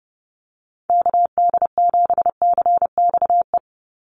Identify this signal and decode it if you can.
Morse code, heard as a keyed tone, about 30 wpm, 700 Hz. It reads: KD7CXE
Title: USA Call Signs 30wpm